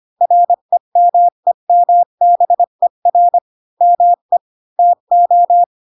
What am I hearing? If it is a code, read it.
Morse code, heard as a keyed tone, about 25 wpm, 700 Hz. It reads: REMEMBER ME TO
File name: Pinocchio / Pinocchio ch7 20 25 ik1hge